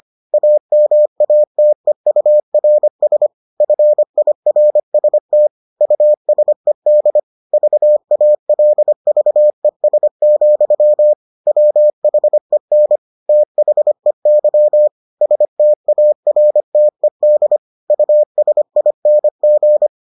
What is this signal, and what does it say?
Morse code, heard as a keyed tone, about 25 wpm, 600 Hz. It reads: AMATEURS FIRST USED VALVES, WHEN THEY STARTED USING